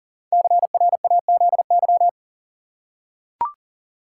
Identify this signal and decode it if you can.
Morse code, heard as a keyed tone, about 39 wpm, 700 Hz. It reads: CRAZY
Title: Five Letter Words - 40wpm